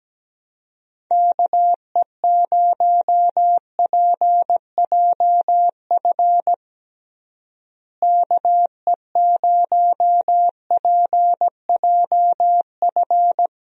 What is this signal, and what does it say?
Morse code, heard as a keyed tone, about 17 wpm, 700 Hz. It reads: KE0PJF KE0PJF
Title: USA Call Signs - 6 Characters - Repeated 3x - 17wpm